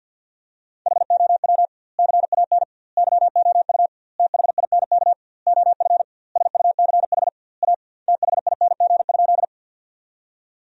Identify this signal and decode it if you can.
Morse code, heard as a keyed tone, about 49 wpm, 700 Hz. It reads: HOW CAN YOU THINK OF SUCH A THING?
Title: 2x Word-Spacing - Sentences from Top 200 Words - 50wpm